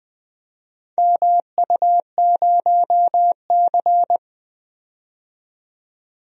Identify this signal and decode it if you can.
Morse code, heard as a keyed tone, about 20 wpm, 700 Hz. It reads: MU0C
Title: DXCC-100 Call Signs - 4 Characters - 20wpm